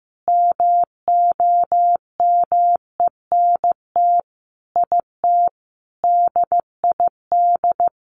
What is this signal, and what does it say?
Morse code, heard as a keyed tone, about 15 wpm, 700 Hz. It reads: MOMENT IT DID
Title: Sentences from Top 1200 Words 15wpm